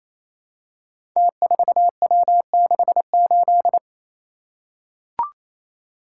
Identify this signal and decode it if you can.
Morse code, heard as a keyed tone, about 28 wpm, 700 Hz. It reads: T4W68
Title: Code Groups - 5 Letter:Numbers - 28wpm